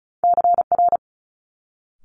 Morse code, heard as a keyed tone, about 36 wpm, 700 Hz. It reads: CR
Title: Multi-letter Phonemes Quick Fire 35wpm